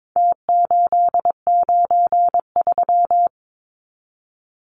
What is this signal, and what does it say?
Morse code, heard as a keyed tone, about 22 wpm, 700 Hz. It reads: T893
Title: Numbers - Contest Style - 4-Digit 22wpm